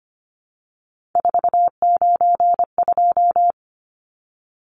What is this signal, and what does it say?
Morse code, heard as a keyed tone, about 25 wpm, 690 Hz. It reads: 492
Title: Numbers - 3-Digit 25wpm